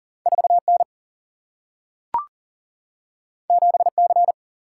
Morse code, heard as a keyed tone, about 39 wpm, 700 Hz. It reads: 4 N 7 C
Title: Single-Letter Single-Number 40wpm